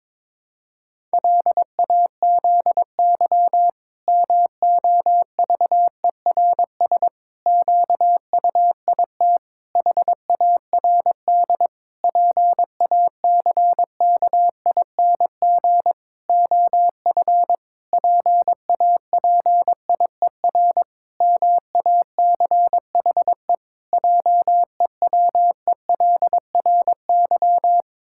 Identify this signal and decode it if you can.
Morse code, heard as a keyed tone, about 22 wpm, 700 Hz. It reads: LAZY MOVERS QUIT HARD PACKING OF PAPIER MACHE JEWELRY
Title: Pangram Challenge v2 - 22wpm